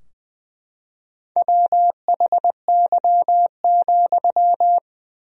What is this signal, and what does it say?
Morse code, heard as a keyed tone, about 20 wpm, 700 Hz. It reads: WHY,
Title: 4x Word-Spacing - Sentences from Top 300 Words - 20wpm